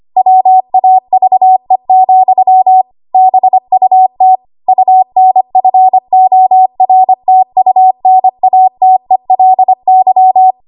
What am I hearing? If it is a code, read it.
Morse code, heard as a keyed tone, about 25 wpm, 750 Hz. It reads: WAVE, BUT UNFORTUNATELY
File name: Archive / 25 WPM / 25WPM